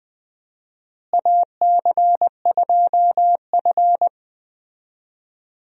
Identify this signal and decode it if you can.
Morse code, heard as a keyed tone, about 20 wpm, 700 Hz. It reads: AC2F
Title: USA Call Signs - 4 Characters - Repeated 3x - 20wpm